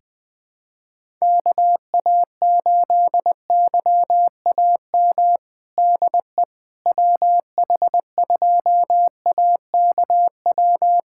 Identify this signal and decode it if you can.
Morse code, heard as a keyed tone, about 20 wpm, 700 Hz. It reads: KA8YAM DE WH2AKW
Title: USA Callsign DE Callsign - 6 Characters - 20wpm